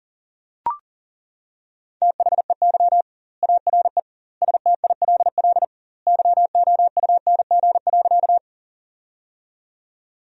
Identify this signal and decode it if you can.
Morse code, heard as a keyed tone, about 40 wpm, 700 Hz. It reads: THEY ARE STILL YOUNG.
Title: 2x Word-Spacing - Sentences from Top 400 Words - 40wpm